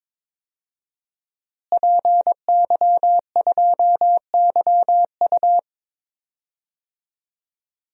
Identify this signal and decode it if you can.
Morse code, heard as a keyed tone, about 22 wpm, 700 Hz. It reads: PY2YU
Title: DXCC-100 Call Signs - 5 Characters - Repeated 3x - 22wpm